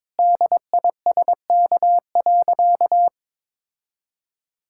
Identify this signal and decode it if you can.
Morse code, heard as a keyed tone, about 22 wpm, 700 Hz. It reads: DISK.
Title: Sentences from Top 1000 Words - Review Part 6 - 22wpm